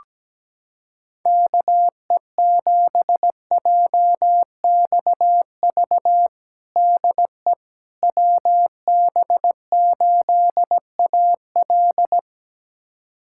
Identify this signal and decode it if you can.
Morse code, heard as a keyed tone, about 17 wpm, 700 Hz. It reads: KE7JXV DE WB8AL
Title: USA Callsign DE Callsign 17wpm